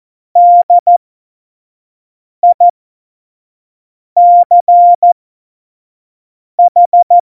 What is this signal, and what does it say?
Morse code, heard as a keyed tone, about 14 wpm, spaced as for 5 wpm, 700 Hz. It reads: DICH
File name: Morsekurs / Witze / Witz 24 April 2018, 25 F70